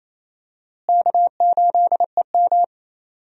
Morse code, heard as a keyed tone, about 28 wpm, 700 Hz. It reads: K8EM
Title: USA Call Signs - 4 Characters - Repeated 3x - 28wpm